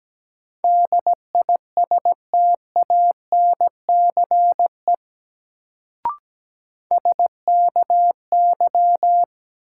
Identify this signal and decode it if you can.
Morse code, heard as a keyed tone, about 17 wpm, 700 Hz. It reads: DISTANCE SKY
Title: Sentences from Top 1200 Words - New Word Drill - 17wpm